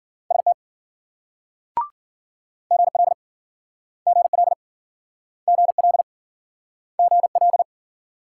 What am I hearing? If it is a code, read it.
Morse code, heard as a keyed tone, about 54 wpm, 700 Hz. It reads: ST GL GL GL GL